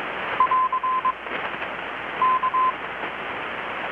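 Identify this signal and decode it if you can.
Morse code, heard as a keyed tone, about 21 wpm, 1030 Hz. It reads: <AR> K
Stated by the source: Morse from KPH's Night of Nights XX maritime radiotelegraphy event, received in Chiapas, Mexico about 3,800 km away on a Kenwood R-600 with a grounded 10 m long-wire antenna.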